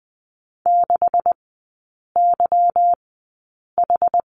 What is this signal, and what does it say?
Morse code, heard as a keyed tone, about 20 wpm, spaced as for 8 wpm, 700 Hz. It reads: 6YH
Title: 191 Lesson 31e - Letter v - Three Characters 20wpm